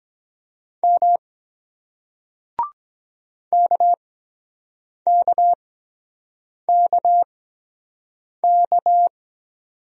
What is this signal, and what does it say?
Morse code, heard as a keyed tone, about 23 wpm, spaced as for 7 wpm, 700 Hz. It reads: M KKKK